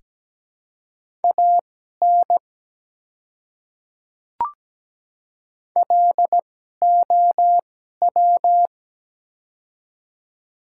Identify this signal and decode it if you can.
Morse code, heard as a keyed tone, about 17 wpm, spaced as for 12 wpm, 700 Hz. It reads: AN LOW
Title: 2x Character Spacing - Top 200 Words - 17wpm